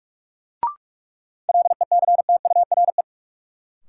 Morse code, heard as a keyed tone, about 46 wpm, 700 Hz. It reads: LECTURE
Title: Sentences from Top 1600 Words - New Word Drill - 45wpm